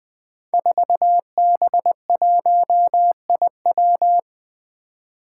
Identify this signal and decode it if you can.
Morse code, heard as a keyed tone, about 20 wpm, 700 Hz. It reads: 4B1IW
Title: Five Challenging Letter-Number Combinations 20wpm